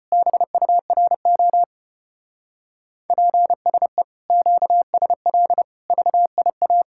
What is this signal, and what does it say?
Morse code, heard as a keyed tone, about 32 wpm, 700 Hz. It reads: BURO PSE QSL VIA